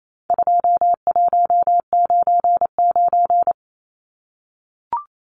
Morse code, heard as a keyed tone, about 28 wpm, 700 Hz. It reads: 2199
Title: Numbers - 4-Digit 28wpm